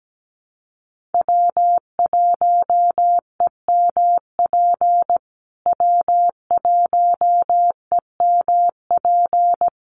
Morse code, heard as a keyed tone, about 17 wpm, 700 Hz. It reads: W1EMP W1EMP
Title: USA Call Signs Repeated 3x 17wpm